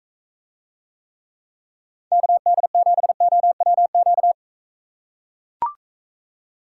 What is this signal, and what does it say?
Morse code, heard as a keyed tone, about 42 wpm, 700 Hz. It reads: KD7OWQ